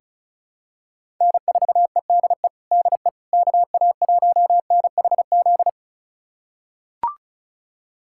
Morse code, heard as a keyed tone, about 36 wpm, 700 Hz. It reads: N4EDE DE KA1NHZ
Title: USA Callsign DE Callsign 35wpm